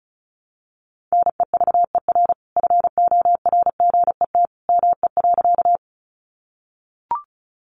Morse code, heard as a keyed tone, about 34 wpm, 700 Hz. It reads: NEVER FORGET ME.
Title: Sentences from Top 1200 Words 35wpm